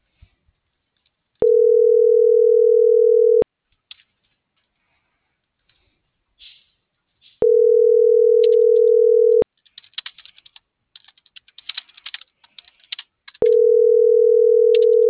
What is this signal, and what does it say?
Ringback tone (North America), 440 + 480 Hz.